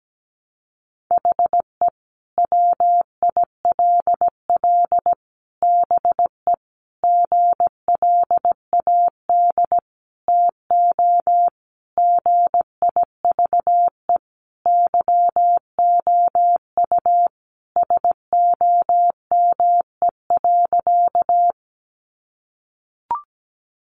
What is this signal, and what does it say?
Morse code, heard as a keyed tone, about 17 wpm, 700 Hz. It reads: HE WILL BE GLAD TO GIVE YOU SOME.
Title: Sentences from Top 800 Words 17wpm